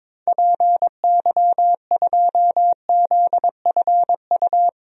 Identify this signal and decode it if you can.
Morse code, heard as a keyed tone, about 22 wpm, 700 Hz. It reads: PY2ZFU
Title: DXCC-100 Call Signs - 6 Characters - 22wpm